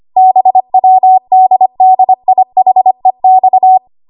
Morse code, heard as a keyed tone, about 25 wpm, 750 Hz. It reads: BWDDIHEX
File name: Archive / 25 WPM / 25WPM